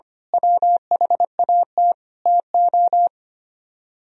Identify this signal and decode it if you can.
Morse code, heard as a keyed tone, about 24 wpm, 700 Hz. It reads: WHAT TO
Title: Speed-Racing - Sets of 5 Words 25wpm